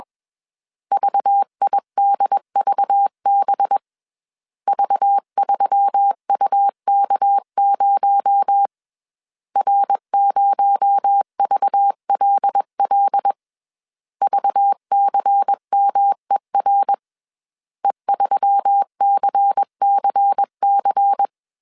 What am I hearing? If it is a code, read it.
Morse code, heard as a keyed tone, about 21 wpm, 800 Hz. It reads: VIDVB V3UK0 R0VLL VCMER E3CCC
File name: CW / DET / 1 6 1500C S B 20wpm 2048